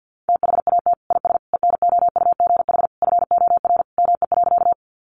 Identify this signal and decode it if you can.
Morse code, heard as a keyed tone, about 50 wpm, 700 Hz. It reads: THAT IS ENOUGH FOR ME.
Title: Sentences from Top 400 Words 50wpm